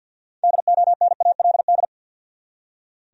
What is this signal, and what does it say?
Morse code, heard as a keyed tone, about 47 wpm, 700 Hz. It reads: DONALD
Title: USA Common Names - 50wpm